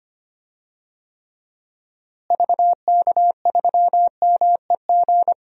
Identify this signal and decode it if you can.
Morse code, heard as a keyed tone, about 25 wpm, 700 Hz. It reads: VK3MEG
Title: DXCC-100 Call Signs - 6 Characters - Repeated 3x - 25wpm